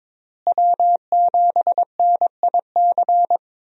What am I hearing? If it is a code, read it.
Morse code, heard as a keyed tone, about 21 wpm, 700 Hz. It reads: W7NIC
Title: USA Call Signs - 5 Characters - Repeated 3x - 22wpm